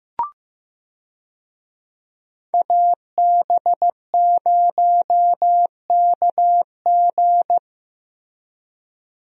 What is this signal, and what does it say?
Morse code, heard as a keyed tone, about 15 wpm, 700 Hz. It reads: AB0KG
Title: USA Call Signs - 5 Characters - Repeated 3x - 15wpm